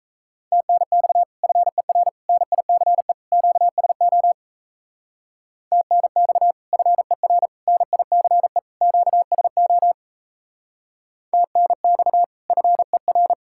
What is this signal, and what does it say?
Morse code, heard as a keyed tone, about 33 wpm, 700 Hz. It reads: TNX FER NICE QSO TNX FER NICE QSO TNX FER
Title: Speed-Racing Common QSO Phrases Encore - 28wpm